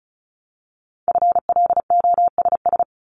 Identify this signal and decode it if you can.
Morse code, heard as a keyed tone, about 36 wpm, 700 Hz. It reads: FLOSS
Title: Multi-letter Phonemes Repeated 3x 35wpm